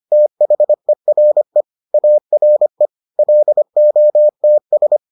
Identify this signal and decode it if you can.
Morse code, heard as a keyed tone, about 25 wpm, 600 Hz. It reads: THERE ARE LOTS